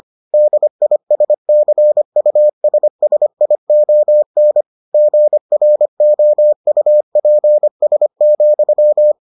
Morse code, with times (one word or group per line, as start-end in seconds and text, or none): 0.3-4.6 DISCUSSION
4.9-9.1 GROUPS,